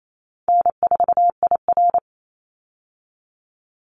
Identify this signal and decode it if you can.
Morse code, heard as a keyed tone, about 28 wpm, 700 Hz. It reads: N4IR
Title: Call Signs Repeated 3x - 28wpm